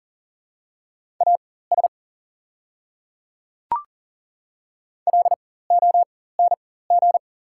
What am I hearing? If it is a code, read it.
Morse code, heard as a keyed tone, about 40 wpm, spaced as for 19 wpm, 690 Hz. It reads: AS LONG
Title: 4x Character Spacing - Top 100 Words - 40wpm